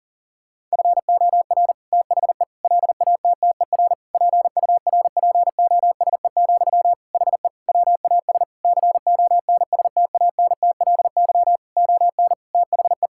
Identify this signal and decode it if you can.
Morse code, heard as a keyed tone, about 40 wpm, 700 Hz. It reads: FOR THE LATTER PURPOSE, HE WAS CONSTANTLY ON THE
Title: Sentences from Top 1600 Words 40wpm